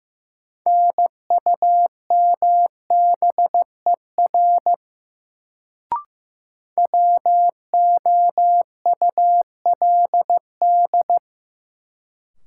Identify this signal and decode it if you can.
Morse code, heard as a keyed tone, about 15 wpm, 700 Hz. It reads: NUMBER WOULD
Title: Sentences from Top 100 Words - New Word Drill - 15wpm